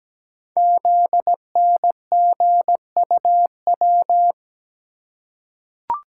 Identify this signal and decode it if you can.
Morse code, heard as a keyed tone, about 17 wpm, 700 Hz. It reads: ZNGUW
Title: Code Groups - 5 Letters - 17wpm